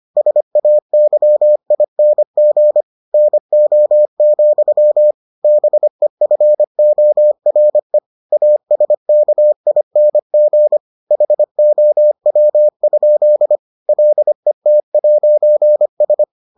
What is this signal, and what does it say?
Morse code, heard as a keyed tone, about 25 wpm, 600 Hz. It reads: SAYING NO, BEFORE ASKING HOW? LET'S